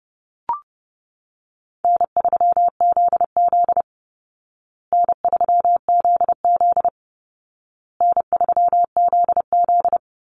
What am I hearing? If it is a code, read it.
Morse code, heard as a keyed tone, about 30 wpm, 700 Hz. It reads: N3ZZ N3ZZ N3ZZ
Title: Call Signs Repeated 2x - 30wpm